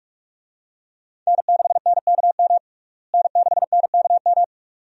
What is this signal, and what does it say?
Morse code, heard as a keyed tone, about 41 wpm, 700 Hz. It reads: N6NKM N6NKM
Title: USA Call Signs - 5 Characters - Repeated 2x - 45wpm